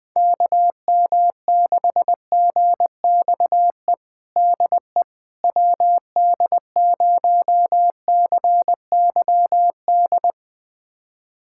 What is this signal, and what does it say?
Morse code, heard as a keyed tone, about 20 wpm, 700 Hz. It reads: KM6GXE DE WD0CYD
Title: USA Callsign DE Callsign - 6 Characters - 20wpm